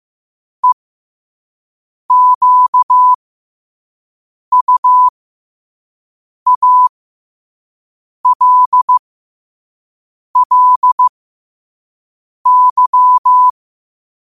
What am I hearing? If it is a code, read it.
Morse code, heard as a keyed tone, about 15 wpm, spaced as for 5 wpm, 1000 Hz. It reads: EQUALLY
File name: qotdfiles / 5wpm